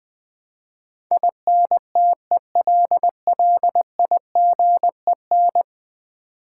Morse code, heard as a keyed tone, about 20 wpm, 700 Hz. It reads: INTELLIGEN
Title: Build A Word - 5 Letters At A Time - 12-Letter Words - 20wpm